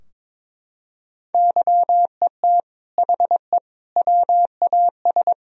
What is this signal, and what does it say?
Morse code, heard as a keyed tone, about 22 wpm, 700 Hz. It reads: YET HE WAS
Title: Sentences from Top 1000 Words - Review Part 3 - 22wpm